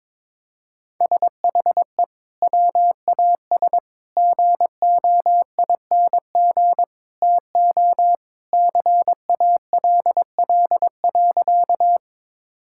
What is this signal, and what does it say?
Morse code, heard as a keyed tone, about 22 wpm, 700 Hz. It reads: SHE WAS GOING TO CALL.
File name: practice / 22wpm / Sentences with 3-Word Phrases - 22wpm